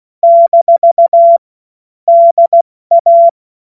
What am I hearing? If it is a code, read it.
Morse code, heard as a keyed tone, about 16 wpm, 680 Hz. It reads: - DA
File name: Morsekurs / Witze / Witz 15. Juni 2021, 70 F80